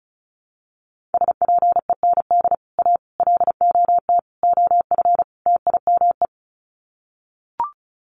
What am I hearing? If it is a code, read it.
Morse code, heard as a keyed tone, about 34 wpm, 700 Hz. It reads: SPEND A LOT OF TIME
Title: Sets of 5 Words 35wpm